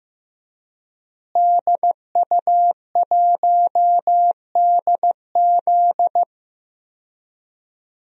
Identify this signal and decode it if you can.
Morse code, heard as a keyed tone, about 14 wpm, 700 Hz. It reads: DU1DZ
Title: DXCC-100 Call Signs - 5 Characters - Repeated 3x - 15wpm